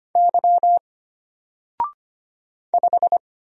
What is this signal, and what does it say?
Morse code, heard as a keyed tone, about 25 wpm, 700 Hz. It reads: Y 5
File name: Single-Letter-Number-25wpm